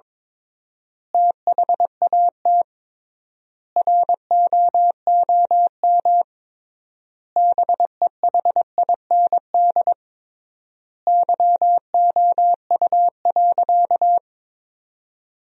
Morse code, heard as a keyed tone, about 22 wpm, 700 Hz. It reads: THAT ROOM BEHIND YOU.
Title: 3x Word-Spacing - Sentences from Top 500 Words - 22wpm